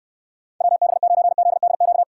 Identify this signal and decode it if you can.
Morse code, heard as a keyed tone, about 64 wpm, 700 Hz. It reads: WB9ZDQ